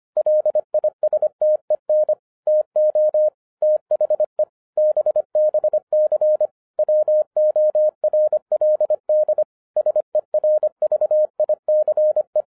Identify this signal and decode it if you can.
Morse code, heard as a keyed tone, about 25 wpm, 610 Hz. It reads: LISTEN TO THE BBC WORLD SERVICE